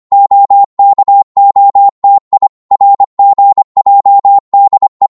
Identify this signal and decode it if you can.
Morse code, heard as a keyed tone, about 25 wpm, 800 Hz. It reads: OKOTI RGJDE
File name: Morse / ltr-25wpm